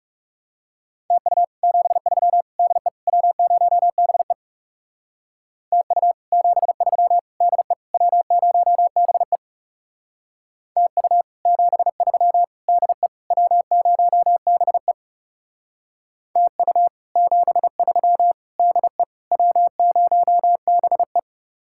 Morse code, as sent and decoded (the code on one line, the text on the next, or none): - ..- / --... ...-- / -.. / .-- ----- -... / - ..- / --... ...-- / -.. / .-- ----- -... / - ..- / --... ...-- / -.. / .-- ----- -... / - ..- / --... ...-- / -.. / .-- ----- -... .
TU 73 DE W0BE TU 73 DE W0BE TU 73 DE W0BE TU 73 DE W0BE